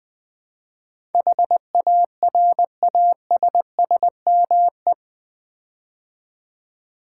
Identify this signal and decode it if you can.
Morse code, heard as a keyed tone, about 20 wpm, 700 Hz. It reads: HARASSME
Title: Build A Word - 4 Letters At A Time - 10-Letter Words - 20wpm